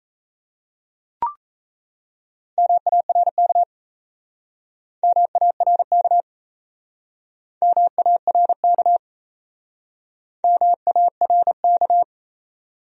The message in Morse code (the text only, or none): MARK MARK MARK MARK